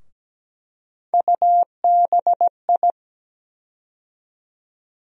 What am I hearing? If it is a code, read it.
Morse code, heard as a keyed tone, about 17 wpm, 700 Hz. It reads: UBI
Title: USA Call Sign Postfixes - 3 Letters 17wpm